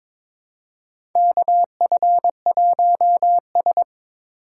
Morse code, heard as a keyed tone, about 22 wpm, 700 Hz. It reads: KF1S
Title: USA Call Signs - 4 Characters - 2x Character Spacing - 22wpm